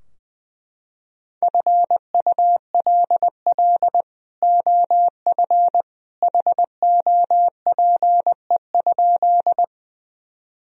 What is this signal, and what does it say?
Morse code, heard as a keyed tone, about 21 wpm, 700 Hz. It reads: FULL OF HOPE?
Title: Sentences from Top 1000 Words - Review Part 3 - 20wpm